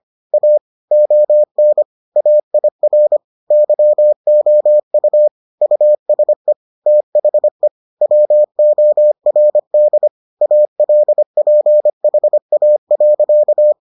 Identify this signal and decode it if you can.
Morse code, heard as a keyed tone, about 25 wpm, 600 Hz. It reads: A ON AIR YOU USE THE WORD ALPHA.